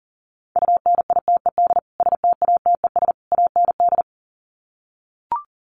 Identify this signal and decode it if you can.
Morse code, heard as a keyed tone, about 40 wpm, 700 Hz. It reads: UNITED STATES AND
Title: Sets of 3 Words 40wpm